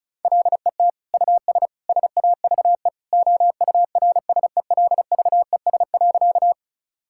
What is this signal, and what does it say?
Morse code, heard as a keyed tone, about 34 wpm, 700 Hz. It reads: LET US SAVE OURSELVES.
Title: Sentences from Top 1600 Words 35wpm